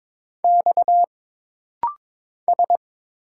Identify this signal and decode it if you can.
Morse code, heard as a keyed tone, about 22 wpm, 700 Hz. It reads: X S
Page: This rapid-fire timing is geared to push your ICR (Instant Character Recognition) skills.